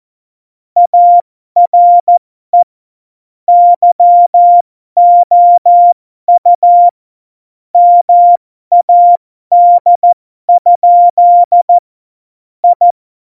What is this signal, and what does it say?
Morse code, heard as a keyed tone, about 14 wpm, spaced as for 12 wpm, 700 Hz. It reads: ARE YOU MAD? I
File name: Morsekurs / Witze